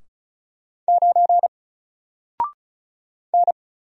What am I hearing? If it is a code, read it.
Morse code, heard as a keyed tone, about 36 wpm, 700 Hz. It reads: 9 N
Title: Single Letter-Number Alt 35wpm